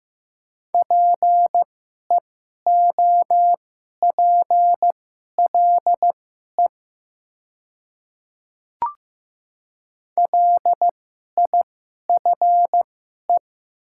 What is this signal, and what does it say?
Morse code, heard as a keyed tone, about 15 wpm, spaced as for 11 wpm, 700 Hz. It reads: PEOPLE LIFE
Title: 2x Character Spacing - Top 500 Words - 15wpm